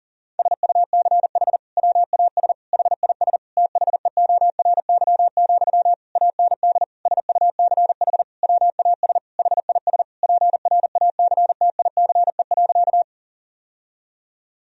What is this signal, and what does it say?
Morse code, heard as a keyed tone, about 40 wpm, 690 Hz. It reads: SUCH WAS HIS THEORY, AND SUCH WAS HIS PRACTICE.
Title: Sentences from Top 1000 Words - Review Part 4 - 40wpm